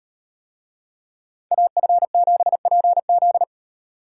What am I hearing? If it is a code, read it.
Morse code, heard as a keyed tone, about 38 wpm, 700 Hz. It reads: AF7PZ